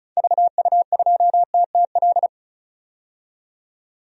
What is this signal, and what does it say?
Morse code, heard as a keyed tone, about 36 wpm, 700 Hz. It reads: VU2TTL
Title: DXCC-100 Call Signs - 6 Characters - 35wpm